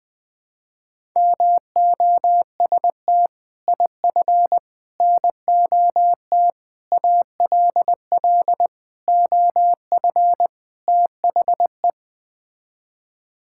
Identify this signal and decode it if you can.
Morse code, heard as a keyed tone, about 20 wpm, 700 Hz. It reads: MOST IF NOT ALL OF THE
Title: Sets of 6 Words - 20wpm